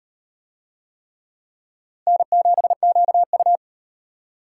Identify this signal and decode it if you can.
Morse code, heard as a keyed tone, about 34 wpm, 700 Hz. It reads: N7QU